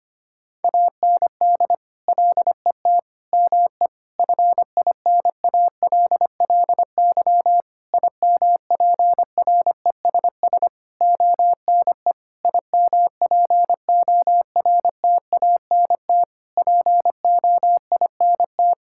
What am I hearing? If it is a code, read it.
Morse code, heard as a keyed tone, about 25 wpm, 700 Hz. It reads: AND LET ME FINALLY IMPRESS ONE IMPORTANT POINT
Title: Sentences from Top 1600 Words 25wpm